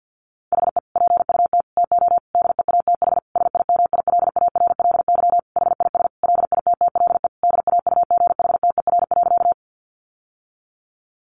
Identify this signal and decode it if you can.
Morse code, heard as a keyed tone, about 48 wpm, 700 Hz. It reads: HE PUT TO DEATH SIMILARLY HIS LITTLE DAUGHTER.
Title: Sentences from Top 1500 Words 50wpm